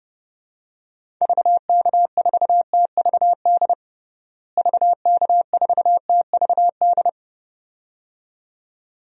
Morse code, heard as a keyed tone, about 30 wpm, 700 Hz. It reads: VK4TVD VK4TVD
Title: DXCC-100 Call Signs - 6 Characters - Repeated 2x - 30wpm